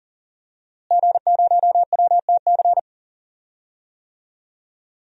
Morse code, heard as a keyed tone, about 40 wpm, 700 Hz. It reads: G0WTC
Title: DXCC-100 Call Signs - 5 Characters - Repeated 2x - 40wpm